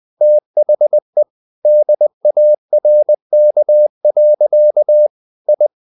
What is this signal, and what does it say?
Morse code, heard as a keyed tone, about 20 wpm, 600 Hz. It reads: THE DARK. I
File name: the old man and the sea / the old man and the sea part 1 20wpm